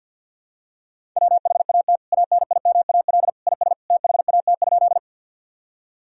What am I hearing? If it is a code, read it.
Morse code, heard as a keyed tone, about 52 wpm, 700 Hz. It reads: WHAT ANIMAL IS THAT?
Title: Sentences from Top 1000 Words - Review Part 1 - 50wpm